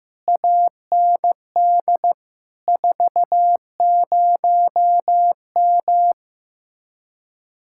Morse code, heard as a keyed tone, about 15 wpm, 700 Hz. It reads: AND 40M
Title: Speed-Racing Common QSO Phrases Encore - 15wpm